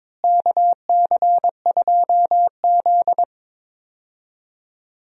Morse code, heard as a keyed tone, about 22 wpm, 700 Hz. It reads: KC2Z
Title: USA Call Signs - 4 Characters - Repeated 2x - 22wpm